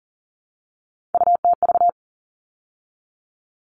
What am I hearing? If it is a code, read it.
Morse code, heard as a keyed tone, about 40 wpm, 700 Hz. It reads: UTV
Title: All Cut Numbers - 3-Digit 40wpm